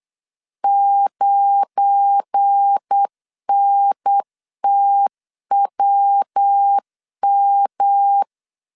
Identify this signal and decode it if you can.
Morse code, heard as a keyed tone, about 8 wpm, 800 Hz. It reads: 9NTWM
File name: LEZIONI / 7 GF9Z / 7 GF9 8wpm 3545